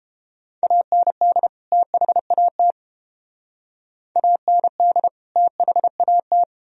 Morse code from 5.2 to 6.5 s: THAT

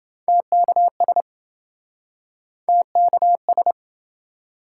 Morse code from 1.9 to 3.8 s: TKS

